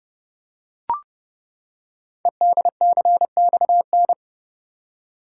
Morse code, heard as a keyed tone, about 28 wpm, 700 Hz. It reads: EDCXN
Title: Code Groups - 5 Letters - 30wpm